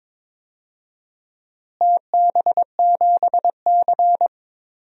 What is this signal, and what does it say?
Morse code, heard as a keyed tone, about 22 wpm, 700 Hz. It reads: TB7C
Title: DXCC-100 Call Signs - 4 Characters - Repeated 3x - 22wpm